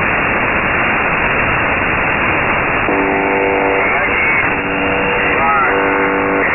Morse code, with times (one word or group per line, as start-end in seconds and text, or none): none